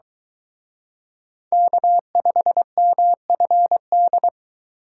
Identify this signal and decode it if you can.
Morse code, heard as a keyed tone, about 23 wpm, 700 Hz. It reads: K5MFD